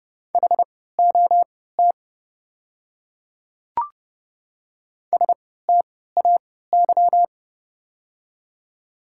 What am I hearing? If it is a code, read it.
Morse code, heard as a keyed tone, about 30 wpm, spaced as for 17 wpm, 700 Hz. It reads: HOT STAY
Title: 3x Character Spacing - Top 500 Words - 30wpm